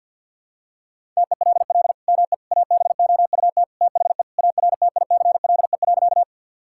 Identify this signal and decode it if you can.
Morse code, heard as a keyed tone, about 49 wpm, 700 Hz. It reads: TELL ME ABOUT THE ARTICLE.
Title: Sentences from Top 1000 Words - Review Part 5 - 50wpm